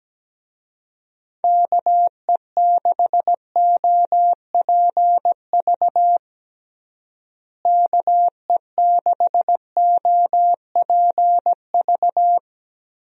Morse code, heard as a keyed tone, about 17 wpm, 700 Hz. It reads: KE6OPV KE6OPV
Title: USA Call Signs - 6 Characters - Repeated 2x - 17wpm